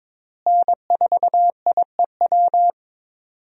Morse code, heard as a keyed tone, about 22 wpm, 700 Hz. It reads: N4IEW
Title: USA Call Signs - 5 Characters - Repeated 2x - 22wpm